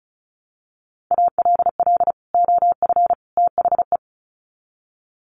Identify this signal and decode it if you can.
Morse code, heard as a keyed tone, about 34 wpm, 700 Hz. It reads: ALL OF THE
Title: Sets of 3 Words 35wpm